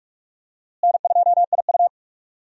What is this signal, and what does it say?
Morse code, heard as a keyed tone, about 46 wpm, 700 Hz. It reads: N2IU
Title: USA Call Signs - 4 Characters - Repeated 2x - 45wpm